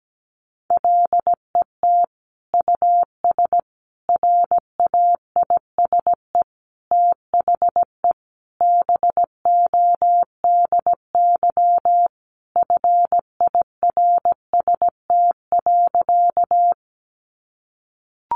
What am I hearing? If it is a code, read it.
Morse code, heard as a keyed tone, about 17 wpm, 700 Hz. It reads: LET US RAISE THE BODY FIRST.
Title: Sentences from Top 1400 Words 17wpm